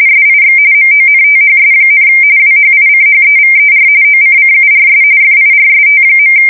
RTTY, about 45.45 baud. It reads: SIX MONTHS, WHICHEVER OCCURS FIRST.
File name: fsk170 / teletype-corp